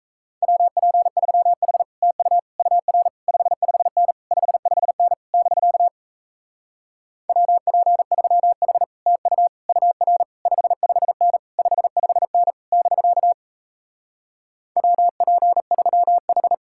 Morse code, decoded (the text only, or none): WP3H TU UR 55N 55N <BK> WP3H TU UR 55N 55N <BK> WP3H